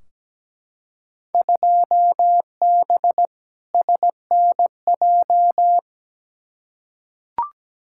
Morse code, heard as a keyed tone, about 17 wpm, 700 Hz. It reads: 2B SNJ
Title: Field Day Exchanges Encore - 17wpm